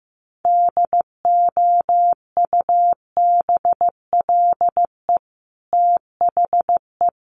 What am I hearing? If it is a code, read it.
Morse code, heard as a keyed tone, about 15 wpm, 700 Hz. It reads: DOUBLE THE
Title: Sentences from Top 1400 Words 15wpm